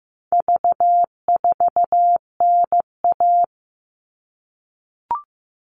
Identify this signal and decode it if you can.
Morse code, heard as a keyed tone, about 15 wpm, 700 Hz. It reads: V4NA
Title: All Cut Numbers - 4-Digit 15wpm